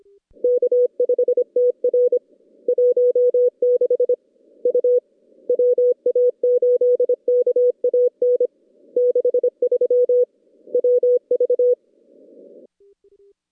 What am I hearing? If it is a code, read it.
Morse code, heard as a keyed tone, about 25 wpm, 490 Hz. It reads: K5TR 16 U WA8KAN 63 WV
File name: contests / CW-SS-k5tr- -2100